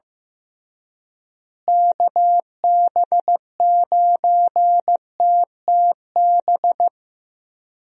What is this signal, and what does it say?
Morse code, heard as a keyed tone, about 15 wpm, 700 Hz. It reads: KB9TTB
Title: USA Call Signs - 6 Characters - Repeated 2x - 15wpm